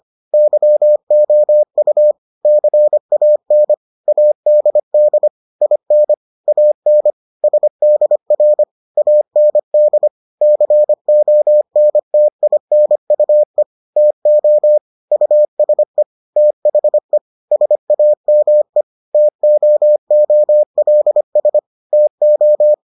Morse code, heard as a keyed tone, about 25 wpm, 600 Hz. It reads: YOU CAN ADD IN AN SDR AND CONTINUE TO USE THE SAME TOOLS TO